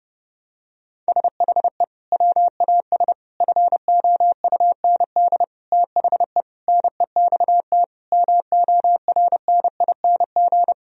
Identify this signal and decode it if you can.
Morse code, heard as a keyed tone, about 30 wpm, 700 Hz. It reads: SHE WAS FOUND THE NEXT MORNING
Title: Sentences from Top 1000 Words - Review Part 3 - 30wpm